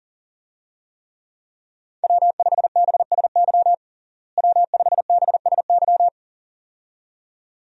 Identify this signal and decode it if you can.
Morse code, heard as a keyed tone, about 40 wpm, 700 Hz. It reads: W5BSY W5BSY
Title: USA Call Signs - 5 Characters - Repeated 2x - 40wpm